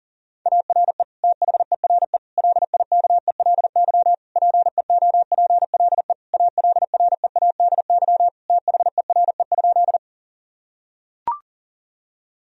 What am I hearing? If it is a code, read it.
Morse code, heard as a keyed tone, about 40 wpm, 700 Hz. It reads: ARE THERE LIKELY PEOPLE ALREADY THERE?
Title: Sentences from Top 1000 Words - Review Part 4 - 40wpm